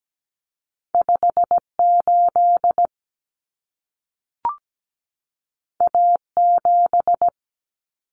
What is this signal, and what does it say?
Morse code, heard as a keyed tone, about 17 wpm, 700 Hz. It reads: 58 A7
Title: Common Cut Numbers - 2-Digit 17wpm